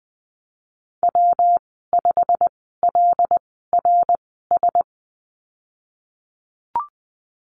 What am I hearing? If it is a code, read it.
Morse code, heard as a keyed tone, about 20 wpm, spaced as for 15 wpm, 700 Hz. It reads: W5LRS
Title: 147 Lesson 24g - Letter y - Call Signs 20wpm